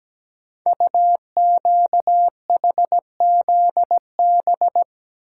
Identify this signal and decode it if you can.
Morse code, heard as a keyed tone, about 17 wpm, 700 Hz. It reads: UQHZB
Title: Code Groups - 5 Letters - 17wpm